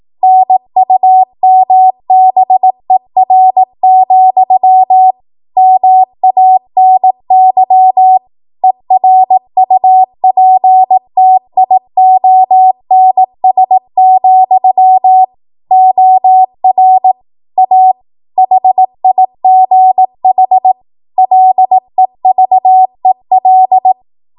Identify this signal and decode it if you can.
Morse code, heard as a keyed tone, about 18 wpm, 750 Hz. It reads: NUMBER, MANY ERUPTIONS, OR A HIGH LEVEL
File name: Morse / 18WPM